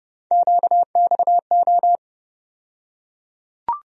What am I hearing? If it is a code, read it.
Morse code, heard as a keyed tone, about 30 wpm, 700 Hz. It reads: QXO
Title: USA Call Sign Postfixes - 3 Letters 30wpm